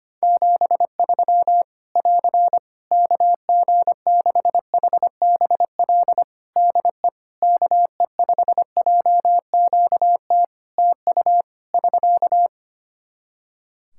Morse code, heard as a keyed tone, about 25 wpm, 700 Hz. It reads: 73 <AR> KG6HBL DE KE5JQT TU <SK>
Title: Practice QSO - Exchange 4 - Ending - 25wpm